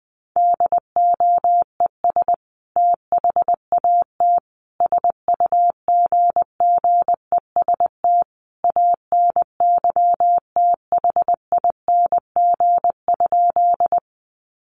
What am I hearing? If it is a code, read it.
Morse code, heard as a keyed tone, about 20 wpm, 700 Hz. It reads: DOES THAT SUGGEST ANYTHING?
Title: Sentences from Top 1300 Words 20wpm